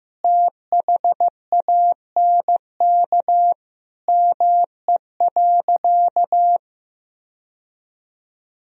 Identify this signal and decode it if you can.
Morse code, heard as a keyed tone, about 15 wpm, 700 Hz. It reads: THANK ME.
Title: Sentences from Top 1000 Words - Review Part 1 - 15wpm